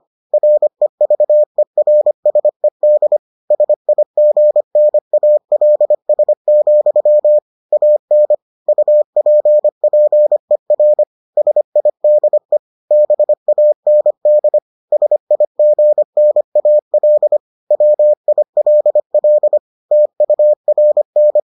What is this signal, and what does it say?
Morse code, heard as a keyed tone, about 25 wpm, 600 Hz. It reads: REVERSED SIGNALS, AN UPPER SIDE BAND SIGNAL WILL TURN